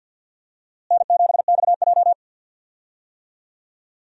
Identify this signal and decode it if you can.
Morse code, heard as a keyed tone, about 49 wpm, 700 Hz. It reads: N7XJ
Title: USA Call Signs - 4 Characters - Repeated 3x - 50wpm